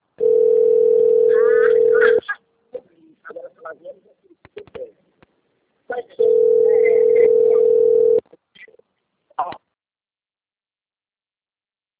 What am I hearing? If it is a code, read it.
Ringback tone (North America), 440 + 480 Hz.